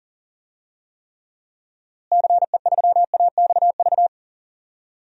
Morse code, heard as a keyed tone, about 37 wpm, 700 Hz. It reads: CE3AXV
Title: DXCC-100 Call Signs - 6 Characters - Repeated 2x - 40wpm